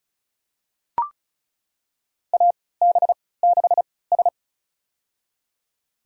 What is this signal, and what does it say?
Morse code, heard as a keyed tone, about 34 wpm, spaced as for 20 wpm, 700 Hz. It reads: AB6S
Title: USA Call Signs - 4 Characters - 3x Character Spacing - 35wpm